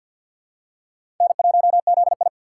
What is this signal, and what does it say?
Morse code, heard as a keyed tone, about 51 wpm, 700 Hz. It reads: N1ZI
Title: USA Call Signs - 6x Character Spacing - 50wpm